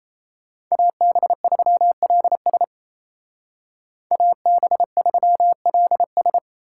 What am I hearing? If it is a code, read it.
Morse code, heard as a keyed tone, about 36 wpm, 700 Hz. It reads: AB3LS AB3LS